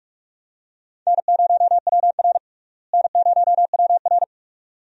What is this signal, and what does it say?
Morse code, heard as a keyed tone, about 44 wpm, 700 Hz. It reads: N0WR N0WR
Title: USA Call Signs - 4 Characters - Repeated 2x - 45wpm